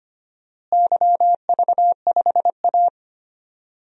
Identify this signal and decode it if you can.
Morse code, heard as a keyed tone, about 25 wpm, 700 Hz. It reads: YV5A
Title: DXCC-100 Call Signs - 4 Characters - Repeated 2x - 25wpm